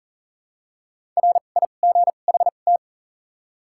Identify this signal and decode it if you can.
Morse code, heard as a keyed tone, about 40 wpm, spaced as for 29 wpm, 700 Hz. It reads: RIGHT
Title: 2x Character Spacing - Top 200 Words - 40wpm